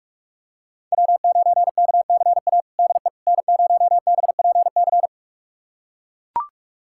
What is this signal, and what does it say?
Morse code, heard as a keyed tone, about 42 wpm, 700 Hz. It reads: W9KCA DE N0BPC
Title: USA Callsign DE Callsign - 5 Characters - 45wpm